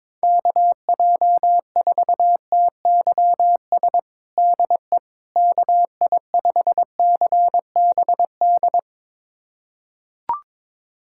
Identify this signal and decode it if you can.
Morse code, heard as a keyed tone, about 22 wpm, 700 Hz. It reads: KJ4TYS DE KI5CBD
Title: USA Callsign DE Callsign - 6 Characters - 22wpm